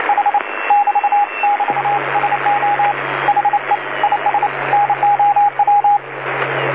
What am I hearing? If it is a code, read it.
Morse code, heard as a keyed tone, about 29 wpm, 820 Hz, with radio noise. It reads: H <BT> <BT> SO HE IS NOW
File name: DC-TRCVR / 80-CW